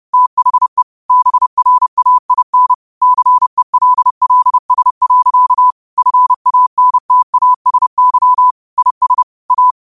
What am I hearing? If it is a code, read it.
Morse code, heard as a keyed tone, about 30 wpm, 1000 Hz. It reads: THE BRAIN CELLS. FANTASY IS A
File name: 30wpm